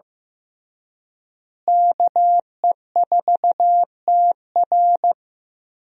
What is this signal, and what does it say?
Morse code, heard as a keyed tone, about 15 wpm, 700 Hz. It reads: KE4TR
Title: USA Call Signs - 5 Characters - Repeated 3x - 15wpm